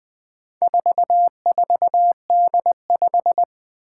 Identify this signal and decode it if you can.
Morse code, heard as a keyed tone, about 20 wpm, 700 Hz. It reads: 44D5
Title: Four Challenging Letter-Number Combinations 20wpm